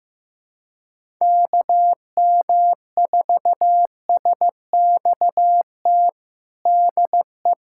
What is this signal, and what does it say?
Morse code, heard as a keyed tone, about 15 wpm, 700 Hz. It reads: KM4SXT DE
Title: USA Callsign DE Callsign - 6 Characters - 15wpm